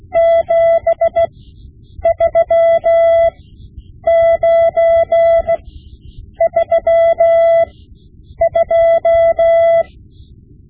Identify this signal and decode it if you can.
Morse code, heard as a keyed tone, about 16 wpm, spaced as for 8 wpm, 670 Hz. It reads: 73932
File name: Morse Lessons / Morse Lessons / Tape-5B